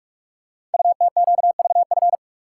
Morse code, heard as a keyed tone, about 44 wpm, 700 Hz. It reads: UTQVF
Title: Code Groups - 5 Letters - 45wpm